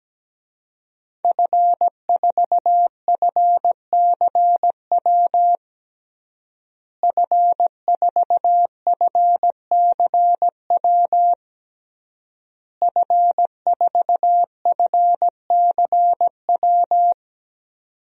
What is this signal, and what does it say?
Morse code, heard as a keyed tone, about 17 wpm, 700 Hz. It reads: F4FCW F4FCW F4FCW
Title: DXCC-100 Call Signs - 5 Characters - Repeated 3x - 17wpm